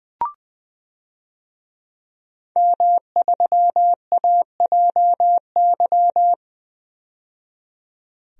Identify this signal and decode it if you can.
Morse code, heard as a keyed tone, about 21 wpm, 700 Hz. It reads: M3AJY
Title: DXCC-100 Call Signs - 5 Characters - 20wpm